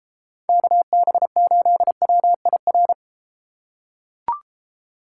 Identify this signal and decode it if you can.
Morse code, heard as a keyed tone, about 34 wpm, 700 Hz. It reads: KB8WIR